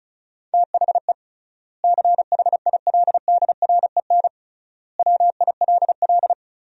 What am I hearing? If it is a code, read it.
Morse code, heard as a keyed tone, about 36 wpm, 700 Hz. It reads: THE CHILDREN WILL
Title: 3x Word-Spacing - Sentences from Top 300 Words - 35wpm